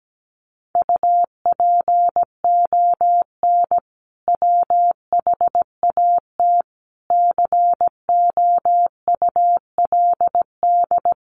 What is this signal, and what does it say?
Morse code, heard as a keyed tone, about 17 wpm, 700 Hz. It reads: UPON WHAT COULD
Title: Sentences from Top 1500 Words 17wpm